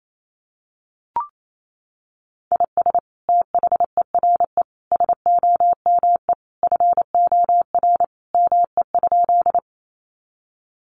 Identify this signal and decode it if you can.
Morse code, heard as a keyed tone, about 28 wpm, 700 Hz. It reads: IS THERE SOME FOR ME?
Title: Sentences from Top 200 Words 28wpm